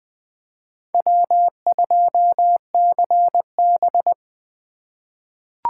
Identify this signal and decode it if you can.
Morse code, heard as a keyed tone, about 21 wpm, 700 Hz. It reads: W2CB
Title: USA Call Signs - 4 Characters - 2x Character Spacing - 20wpm